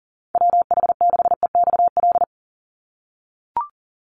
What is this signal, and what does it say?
Morse code, heard as a keyed tone, about 40 wpm, 700 Hz. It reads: WH6EXL
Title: USA Call Signs 40wpm